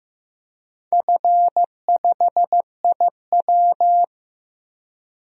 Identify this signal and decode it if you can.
Morse code, heard as a keyed tone, about 15 wpm, 700 Hz. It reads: F5IW
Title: DXCC-100 Call Signs - 4 Characters - Repeated 3x - 15wpm